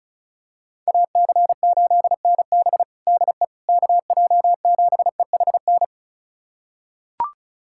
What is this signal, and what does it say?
Morse code, heard as a keyed tone, about 36 wpm, 700 Hz. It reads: AC8NB DE KJ7EHN
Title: USA Callsign DE Callsign 35wpm